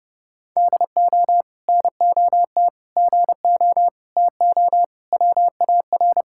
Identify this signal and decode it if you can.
Morse code, heard as a keyed tone, about 30 wpm, 700 Hz. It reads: DO NOT GO TO WAR
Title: Sentences from Top 1000 Words - Review Part 5 - 30wpm